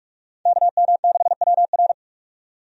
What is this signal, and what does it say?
Morse code, heard as a keyed tone, about 46 wpm, 700 Hz. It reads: KM6WR
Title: USA Call Signs - 5 Characters - 45wpm